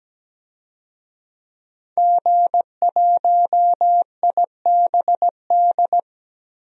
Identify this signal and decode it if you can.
Morse code, heard as a keyed tone, about 17 wpm, 700 Hz. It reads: G1IBD
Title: DXCC-100 Call Signs - 5 Characters - 17wpm